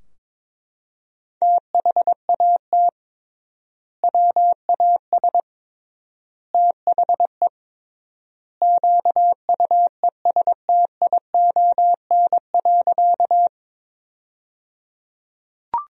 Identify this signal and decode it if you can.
Morse code, heard as a keyed tone, about 22 wpm, 700 Hz. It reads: THAT WAS THE QUESTION.
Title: 3x Word-Spacing - Sentences from Top 400 Words - 22wpm